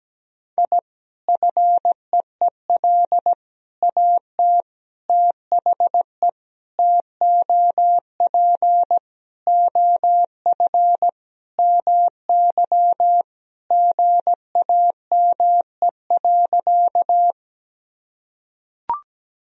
Morse code, heard as a keyed tone, about 17 wpm, 700 Hz. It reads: I FEEL AT THE TOP OF MY GAME.
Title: Sentences from Top 1000 Words - Review Part 2 - 17wpm